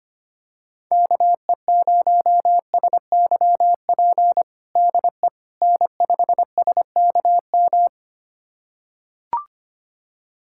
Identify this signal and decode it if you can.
Morse code, heard as a keyed tone, about 25 wpm, 700 Hz. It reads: KE0SYP DE N5SKM
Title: USA Callsign DE Callsign 25wpm